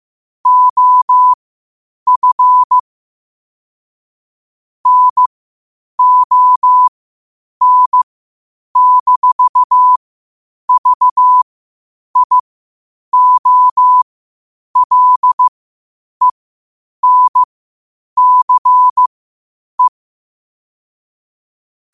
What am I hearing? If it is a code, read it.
Morse code, heard as a keyed tone, about 15 wpm, spaced as for 8 wpm, 1000 Hz. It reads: OF NON-VIOLENCE
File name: qotdfiles / 8wpm